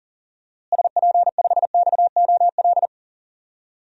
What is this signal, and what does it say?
Morse code, heard as a keyed tone, about 40 wpm, 700 Hz. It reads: SP5XOL
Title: DXCC-100 Call Signs - 6 Characters - Repeated 3x - 40wpm